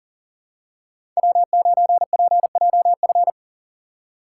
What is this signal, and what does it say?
Morse code, heard as a keyed tone, about 40 wpm, 700 Hz. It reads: W9PJF
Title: USA Call Signs - 5 Characters - 2x Character Spacing - 40wpm